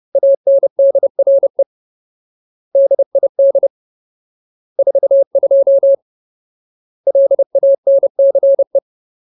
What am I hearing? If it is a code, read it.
Morse code, heard as a keyed tone, about 30 wpm, 550 Hz. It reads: ANDRE DID 42 LANCE
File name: Ultimate Word List / 30WPM Ultimate